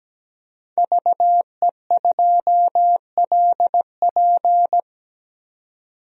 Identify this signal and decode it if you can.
Morse code, heard as a keyed tone, about 17 wpm, 700 Hz. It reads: VE2LP
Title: DXCC-100 Call Signs - 5 Characters - 17wpm